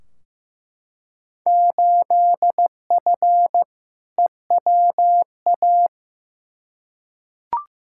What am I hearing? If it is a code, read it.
Morse code, heard as a keyed tone, about 15 wpm, 700 Hz. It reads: 8F EWA
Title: Field Day Exchanges Encore - 15wpm